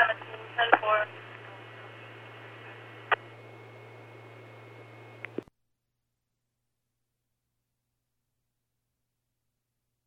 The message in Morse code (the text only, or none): none